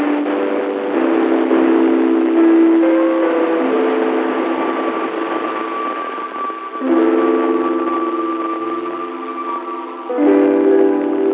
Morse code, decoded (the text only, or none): none